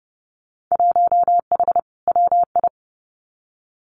Morse code, heard as a keyed tone, about 31 wpm, 700 Hz. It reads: 1H WI
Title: Winter Field Day Exchanges 30wpm